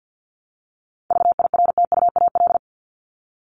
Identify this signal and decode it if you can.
Morse code, heard as a keyed tone, about 50 wpm, 700 Hz. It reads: VIRTUAL
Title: LY Suffix - Top 100 Words with Base Words 50wpm